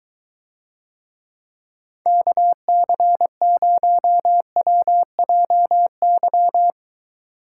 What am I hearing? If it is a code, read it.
Morse code, heard as a keyed tone, about 23 wpm, 700 Hz. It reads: KC0WJY